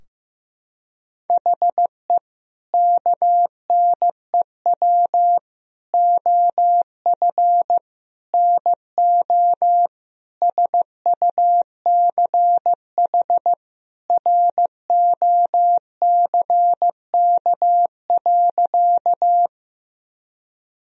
Morse code, heard as a keyed tone, about 15 wpm, 700 Hz. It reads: HE KNEW OF NO SUCH ROCK.
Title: Sentences from Top 1000 Words - Review Part 2 - 15wpm